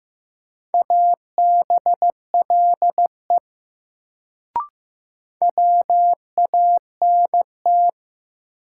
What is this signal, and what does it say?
Morse code, heard as a keyed tone, about 15 wpm, 700 Hz. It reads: ABLE WANT
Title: Sets of 2 Words - New Word Drill - 15wpm